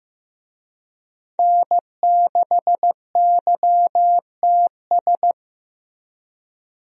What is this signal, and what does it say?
Morse code, heard as a keyed tone, about 15 wpm, 700 Hz. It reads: N6YTS
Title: USA Call Signs - 5 Characters - Repeated 3x - 15wpm